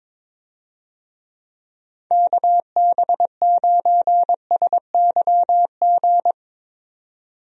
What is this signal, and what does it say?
Morse code, heard as a keyed tone, about 22 wpm, 700 Hz. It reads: KB9SYG
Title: USA Call Signs - 6 Characters - 22wpm